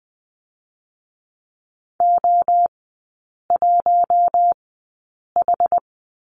Morse code, heard as a keyed tone, about 20 wpm, spaced as for 8 wpm, 700 Hz. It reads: O1H
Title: 060 Lesson 11e - Letter h - Three Characters 20wpm